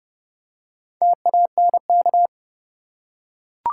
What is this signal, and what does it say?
Morse code, heard as a keyed tone, about 30 wpm, 700 Hz. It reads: TANK
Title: Four Letter Words - 30wpm